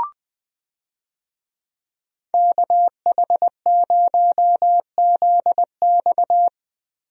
Morse code, heard as a keyed tone, about 20 wpm, 700 Hz. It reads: KH0ZX
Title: USA Call Signs - 5 Characters - Repeated 3x - 20wpm